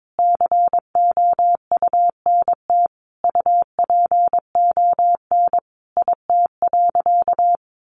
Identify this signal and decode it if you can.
Morse code, heard as a keyed tone, about 22 wpm, 700 Hz. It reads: COUNT UPON IT.
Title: Sentences from Top 1500 Words 22wpm